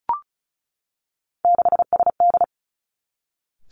Morse code, heard as a keyed tone, about 36 wpm, 700 Hz. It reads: 6SD
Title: Three Challenging Letter-Number Combinations 35wpm